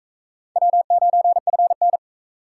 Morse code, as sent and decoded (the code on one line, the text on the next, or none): .-- ----. ..-. -.
W9FN